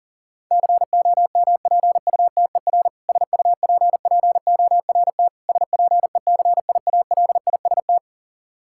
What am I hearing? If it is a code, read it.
Morse code, heard as a keyed tone, about 40 wpm, 700 Hz. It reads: COMPUTER SUPPORT SPECIALIST
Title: Practice QSO - Exchange 3 - More Info - 40wpm